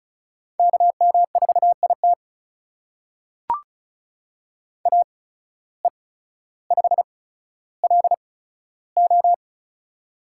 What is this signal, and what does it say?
Morse code, heard as a keyed tone, about 36 wpm, spaced as for 15 wpm, 700 Hz. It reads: KM4IT AE5LO
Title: USA Call Signs - 5 Characters - 8x Character Spacing - 35wpm